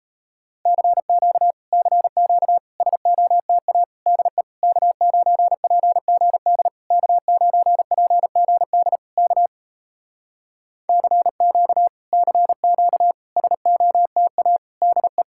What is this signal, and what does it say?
Morse code, heard as a keyed tone, about 38 wpm, 700 Hz. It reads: CQ CQ SOTA DE K9PGD K9PGD K CQ CQ SOTA DE